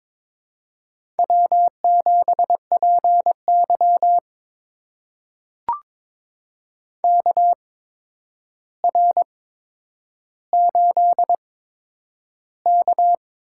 Morse code, heard as a keyed tone, about 22 wpm, spaced as for 9 wpm, 700 Hz. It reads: W7PY KR8K
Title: USA Call Signs - 4 Characters - 8x Character Spacing - 22wpm